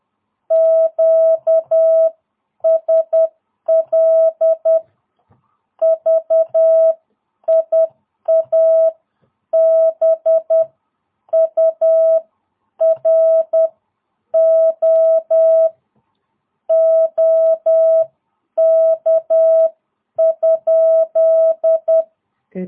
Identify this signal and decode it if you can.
Morse code, heard as a keyed tone, about 10 wpm, spaced as for 8 wpm, 660 Hz. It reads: QSL VIA BURO OK?
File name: jh8cbh / 2022cw